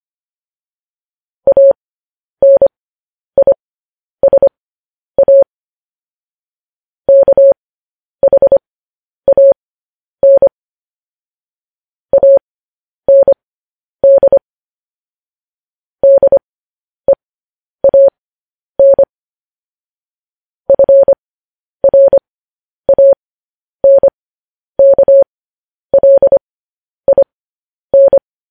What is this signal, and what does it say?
Morse code, heard as a keyed tone, about 25 wpm, spaced as for 10 wpm, 550 Hz. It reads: ANISA KHAN AND DEAN FRANKLIN
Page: news headlines in morse code at 10 wpm